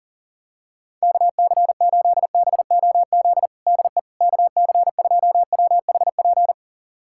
Morse code, heard as a keyed tone, about 40 wpm, 700 Hz. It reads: KC8BOZ DE KC2WHP
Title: USA Callsign DE Callsign - 6 Characters - 40wpm